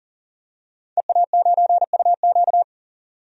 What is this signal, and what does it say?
Morse code, heard as a keyed tone, about 40 wpm, 700 Hz. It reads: EA9UQ
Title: DXCC-100 Call Signs - 5 Characters - 40wpm